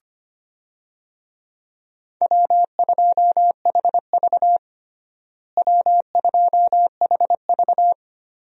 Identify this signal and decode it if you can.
Morse code, heard as a keyed tone, about 25 wpm, 700 Hz. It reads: W2HV W2HV
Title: USA Call Signs - 4 Characters - Repeated 3x - 25wpm